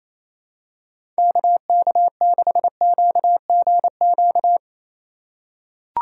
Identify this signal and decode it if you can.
Morse code, heard as a keyed tone, about 28 wpm, 700 Hz. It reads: KK6QGQ
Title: USA Call Signs - 6 Characters - 3x Character Spacing - 28wpm